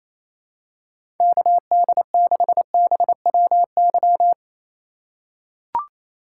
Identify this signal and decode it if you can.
Morse code, heard as a keyed tone, about 28 wpm, 700 Hz. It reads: KD6BWY
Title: USA Call Signs - 6 Characters - 2x Character Spacing - 28wpm